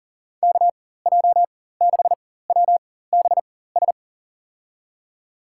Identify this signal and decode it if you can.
Morse code, heard as a keyed tone, about 38 wpm, spaced as for 18 wpm, 700 Hz. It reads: KJ6WBS
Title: USA Call Signs - 4x Character Spacing - 40wpm